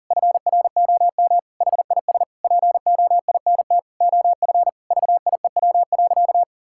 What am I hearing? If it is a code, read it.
Morse code, heard as a keyed tone, about 40 wpm, 700 Hz. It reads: FROM HIS POINT OF VIEW.
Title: Sentences from Top 1000 Words - Review Part 6 - 40wpm